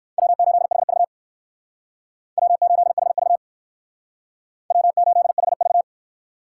Morse code, heard as a keyed tone, about 66 wpm, 700 Hz. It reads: W8HV W8HV W8HV